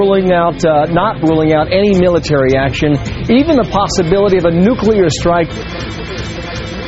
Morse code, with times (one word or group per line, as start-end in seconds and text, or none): none